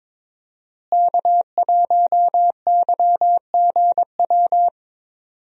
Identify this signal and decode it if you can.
Morse code, heard as a keyed tone, about 22 wpm, 700 Hz. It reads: K1YGW
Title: USA Call Signs - 5 Characters - Repeated 3x - 22wpm